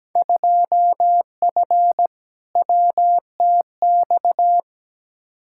Morse code, heard as a keyed tone, about 17 wpm, 700 Hz. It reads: 2F WTX
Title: Field Day Exchanges - 17wpm